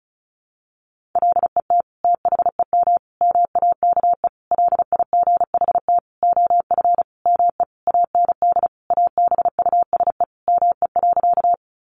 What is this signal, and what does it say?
Morse code, heard as a keyed tone, about 34 wpm, 700 Hz. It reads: LET THEM MAKE LIGHT OF ME AND ABUSE ME.
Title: Sentences from Top 1300 Words 35wpm